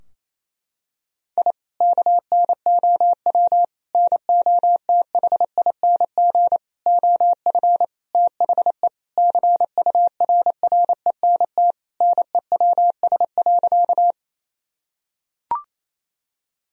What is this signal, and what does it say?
Morse code, heard as a keyed tone, about 28 wpm, 700 Hz. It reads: I KNOW NOTHING OF THE CURRENT NEWS.
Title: Sentences from Top 1000 Words - Review Part 4 - 28wpm